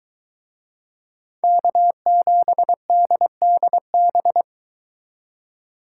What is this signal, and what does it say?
Morse code, heard as a keyed tone, about 22 wpm, 700 Hz. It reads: K7DDB